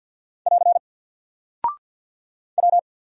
Morse code, heard as a keyed tone, about 50 wpm, 700 Hz. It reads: <AR> W
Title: Single Character (Letter:Number:Punctuation:Prosign) 50wpm